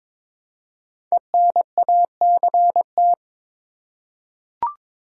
Morse code, heard as a keyed tone, about 22 wpm, 700 Hz. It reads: ENACT
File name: 5 Letter Words - First 3 Letters Given - 22wpm